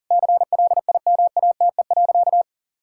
Morse code, heard as a keyed tone, about 40 wpm, 700 Hz. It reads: CLIMATE.
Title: Sentences from Top 1600 Words 40wpm